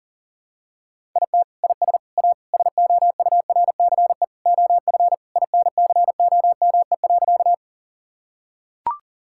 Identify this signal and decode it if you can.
Morse code, heard as a keyed tone, about 40 wpm, 700 Hz. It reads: IT IS A SOURCE OF INCOME.
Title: Sentences from Top 1000 Words - Review Part 5 - 40wpm